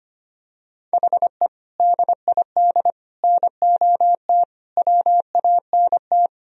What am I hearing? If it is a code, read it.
Morse code, heard as a keyed tone, about 25 wpm, 700 Hz. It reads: HE DID NOT WANT
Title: Sentences from Top 1000 Words - Review Part 1 - 25wpm